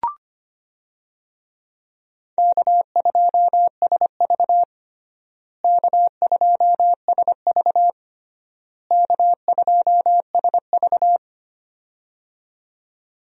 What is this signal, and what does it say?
Morse code, heard as a keyed tone, about 25 wpm, 700 Hz. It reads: K2SV K2SV K2SV
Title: USA Call Signs - 4 Characters - Repeated 3x - 25wpm